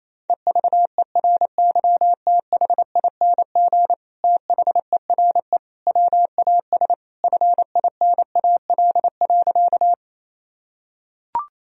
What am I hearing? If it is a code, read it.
Morse code, heard as a keyed tone, about 28 wpm, 700 Hz. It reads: EVERYTHING THERE WAS FINAL.
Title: Sentences from Top 1000 Words - Review Part 2 - 28wpm